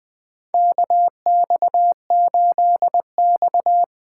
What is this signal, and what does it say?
Morse code, heard as a keyed tone, about 20 wpm, 700 Hz. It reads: KX8X
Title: K1USN SST QSO Practice - 20wpm